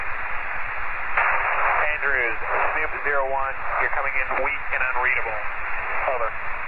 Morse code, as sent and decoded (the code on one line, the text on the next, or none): none